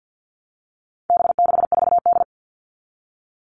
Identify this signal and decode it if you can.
Morse code, heard as a keyed tone, about 50 wpm, 700 Hz. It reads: B64D
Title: All Cut Numbers - 4-Digit 50wpm